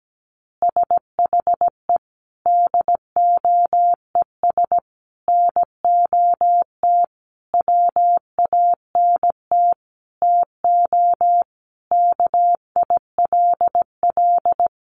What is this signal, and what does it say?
Morse code, heard as a keyed tone, about 17 wpm, 700 Hz. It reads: SHE DOES NOT WANT TO KILL
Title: Easy Phrases 17wpm